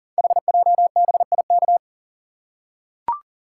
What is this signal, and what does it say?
Morse code, heard as a keyed tone, about 40 wpm, 700 Hz. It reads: HJBIK
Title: Code Groups - 5 Letters - 40wpm